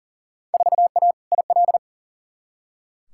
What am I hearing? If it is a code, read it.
Morse code, heard as a keyed tone, about 40 wpm, 700 Hz. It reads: 4A IL
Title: Field Day Exchanges - 40wpm